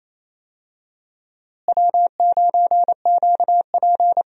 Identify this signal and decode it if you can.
Morse code, heard as a keyed tone, about 28 wpm, 700 Hz. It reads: W9QP
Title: USA Call Signs - 4 Characters - Repeated 3x - 28wpm